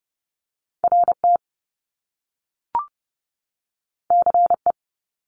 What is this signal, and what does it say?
Morse code, heard as a keyed tone, about 30 wpm, 700 Hz. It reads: RT CE
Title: Top Two Letter Combinations in Words 30wpm